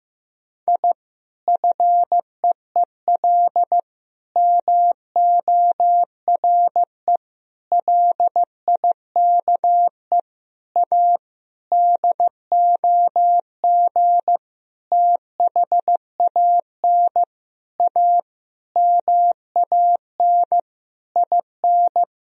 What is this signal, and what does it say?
Morse code, heard as a keyed tone, about 16 wpm, 700 Hz. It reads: I FEEL MORE LIKE A DOG THAN A MAN IN
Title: Sentences from Top 1000 Words - Review Part 2 - 15wpm